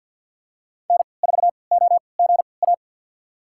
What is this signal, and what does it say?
Morse code, heard as a keyed tone, about 51 wpm, spaced as for 28 wpm, 700 Hz. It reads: N4OGA
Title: USA Call Signs - 5 Characters - 3x Character Spacing - 50wpm